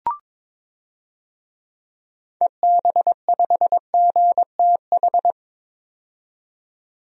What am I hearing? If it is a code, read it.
Morse code, heard as a keyed tone, about 21 wpm, 700 Hz. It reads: EB5GTH
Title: DXCC-100 Call Signs - 6 Characters - 22wpm